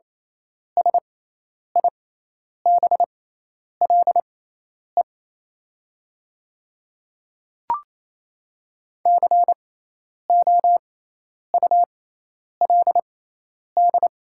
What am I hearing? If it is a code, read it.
Morse code, heard as a keyed tone, about 28 wpm, spaced as for 10 wpm, 700 Hz. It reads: SIBLE COULD
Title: 6x Character Spacing - Top 500 Words - 28wpm